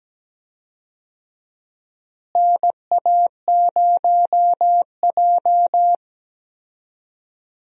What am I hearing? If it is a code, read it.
Morse code, heard as a keyed tone, about 17 wpm, 700 Hz. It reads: NA0J
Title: USA Call Signs - 4 Characters - Repeated 2x - 17wpm